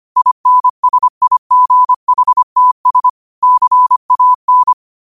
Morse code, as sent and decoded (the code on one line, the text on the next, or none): .. -. ... .. --. .... - ... / -.-. .- -.
INSIGHTS CAN